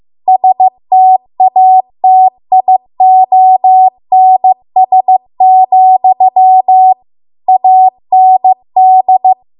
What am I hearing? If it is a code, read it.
Morse code, heard as a keyed tone, about 15 wpm, 750 Hz. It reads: STATIONS, AND